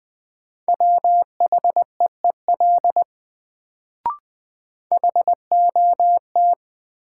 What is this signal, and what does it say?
Morse code, heard as a keyed tone, about 20 wpm, 700 Hz. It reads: WHEEL HOT
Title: Sentences from Top 500 Words - New Word Drill - 20wpm